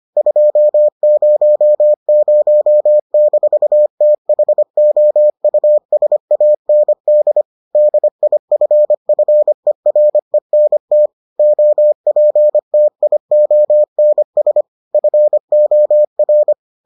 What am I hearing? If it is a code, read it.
Morse code, heard as a keyed tone, about 25 wpm, 600 Hz. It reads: 200-THOUSAND DIFFERENT OPTIONS FOR